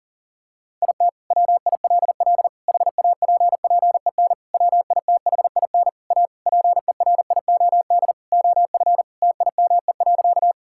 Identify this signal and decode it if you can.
Morse code, heard as a keyed tone, about 40 wpm, 700 Hz. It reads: IT WILL HAPPEN WITHIN A PERIOD OF TIME.
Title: Sentences from Top 1000 Words - Review Part 5 - 40wpm